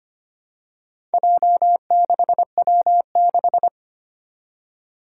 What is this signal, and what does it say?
Morse code, heard as a keyed tone, about 25 wpm, 700 Hz. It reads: J6W6
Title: Four Challenging Letter-Number Combinations 25wpm